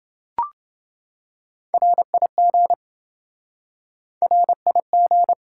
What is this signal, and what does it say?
Morse code, heard as a keyed tone, about 28 wpm, 700 Hz. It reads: RIG RIG